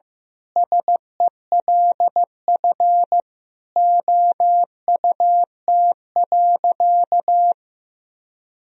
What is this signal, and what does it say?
Morse code, heard as a keyed tone, about 16 wpm, 700 Hz. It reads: SELF OUT.
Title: Sentences from Top 1000 Words - Review Part 1 - 15wpm